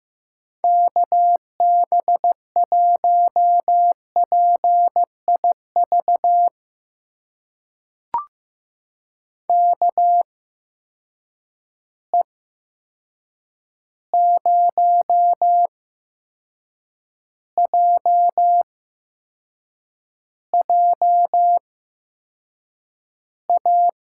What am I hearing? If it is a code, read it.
Morse code, heard as a keyed tone, about 15 wpm, spaced as for 6 wpm, 700 Hz. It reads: KB1PIV KE0JJA
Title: USA Call Signs - 6 Characters - 8x Character Spacing - 15wpm